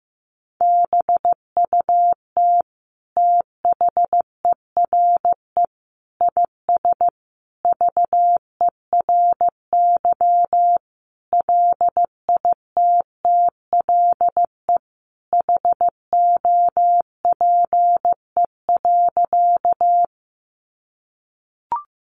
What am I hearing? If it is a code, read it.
Morse code, heard as a keyed tone, about 14 wpm, 700 Hz. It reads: BUT THERE IS VERY LITTLE HOPE.
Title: Sentences from Top 1000 Words 15wpm